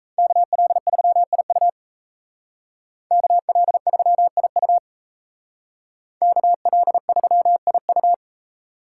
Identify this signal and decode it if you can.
Morse code, heard as a keyed tone, about 36 wpm, 700 Hz. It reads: KL3IU KL3IU KL3IU